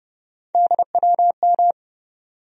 Morse code, heard as a keyed tone, about 31 wpm, 700 Hz. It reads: DWM
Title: USA Call Sign Postfixes - 3 Letters 30wpm